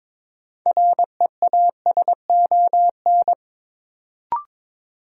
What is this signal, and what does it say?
Morse code, heard as a keyed tone, about 22 wpm, 700 Hz. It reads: REASON
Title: Sentences from Top 700 Words - New Word Drill - 22wpm